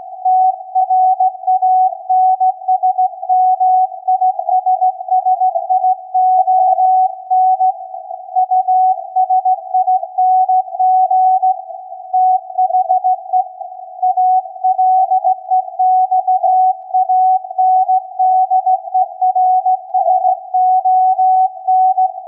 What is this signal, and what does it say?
Morse code, heard as a keyed tone, about 16 wpm, 740 Hz. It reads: TRANSMISSION USING THE ALEXANDERSON
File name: iz7slz / VLF / SAQ 16-11-2022